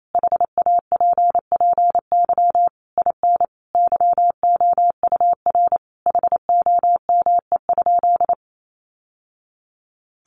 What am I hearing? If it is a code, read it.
Morse code, heard as a keyed tone, about 28 wpm, 700 Hz. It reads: HAPPY IN YOUR HOME?
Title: Sentences from Top 1000 Words 28wpm